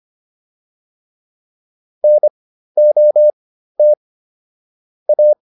Morse code, heard as a keyed tone, about 25 wpm, spaced as for 13 wpm, 600 Hz. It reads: NOT A